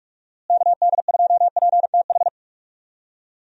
Morse code, heard as a keyed tone, about 41 wpm, 700 Hz. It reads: KD2PTH
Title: USA Call Signs - 6 Characters - Repeated 3x - 45wpm